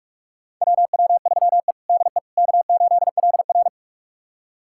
Morse code, heard as a keyed tone, about 46 wpm, 700 Hz. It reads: WW3E DE K8LR
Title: USA Callsign DE Callsign - 4 Characters - 45wpm